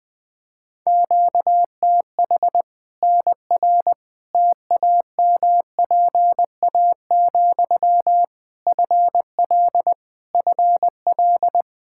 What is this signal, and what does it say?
Morse code, heard as a keyed tone, about 20 wpm, 700 Hz. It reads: QTH NR TAMPA, FL FL
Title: K1USN SST QSO Practice 20wpm